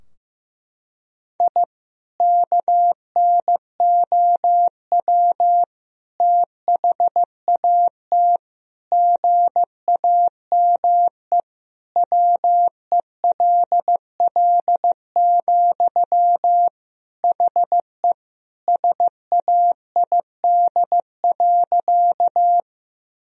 Morse code, heard as a keyed tone, about 15 wpm, 700 Hz. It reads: I KNOW THAT GAME WELL, HE SAID.
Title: Sentences from Top 1000 Words - Review Part 3 - 15wpm